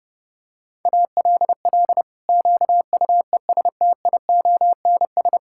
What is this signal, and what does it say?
Morse code, heard as a keyed tone, about 30 wpm, 700 Hz. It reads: ALL QUESTIONS
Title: Pangram Challenge v2 - 30wpm